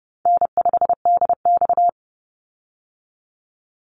Morse code, heard as a keyed tone, about 30 wpm, 700 Hz. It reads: N5DX
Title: Call Signs Repeated 3x - 30wpm